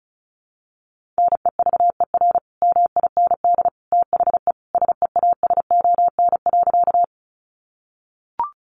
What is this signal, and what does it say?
Morse code, heard as a keyed tone, about 34 wpm, 700 Hz. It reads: NEVER MIND THE SEASON.
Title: Sentences from Top 1400 Words 35wpm